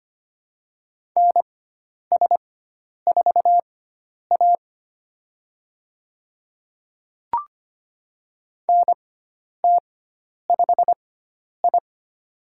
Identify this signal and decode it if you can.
Morse code, heard as a keyed tone, about 25 wpm, spaced as for 10 wpm, 700 Hz. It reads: NS4A NT5I
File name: Lesson 16g - Letter c - Call Signs 25wpm